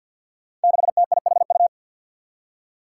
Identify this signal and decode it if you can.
Morse code, heard as a keyed tone, about 47 wpm, 700 Hz. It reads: 6TIHU
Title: Five Challenging Letter-Number Combinations 50wpm